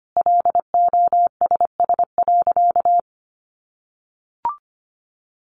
Morse code, heard as a keyed tone, about 25 wpm, 700 Hz. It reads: LOSS.
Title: Sentences from Top 1400 Words 25wpm